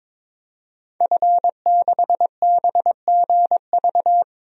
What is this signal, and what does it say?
Morse code, heard as a keyed tone, about 21 wpm, 700 Hz. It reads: F6BGV
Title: DXCC-100 Call Signs - 5 Characters - 22wpm